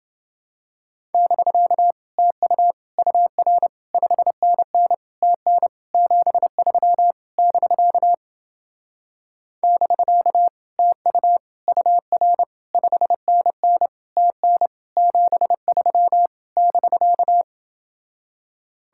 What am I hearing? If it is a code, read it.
Morse code, heard as a keyed tone, about 28 wpm, 700 Hz. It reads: <BK> TU UR 5NN TN 73 <BK> <BK> TU UR 5NN TN 73 <BK>